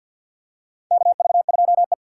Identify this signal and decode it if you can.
Morse code, heard as a keyed tone, about 53 wpm, 700 Hz. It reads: KV2E
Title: USA Call Signs - 4 Characters - 6x Character Spacing - 50wpm